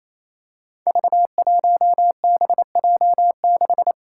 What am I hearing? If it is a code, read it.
Morse code, heard as a keyed tone, about 28 wpm, 700 Hz. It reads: V1BJ6
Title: Five Challenging Letter-Number Combinations 28wpm